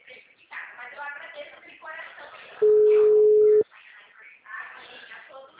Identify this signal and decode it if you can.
Ringback tone (Europe, 425 Hz), with speech.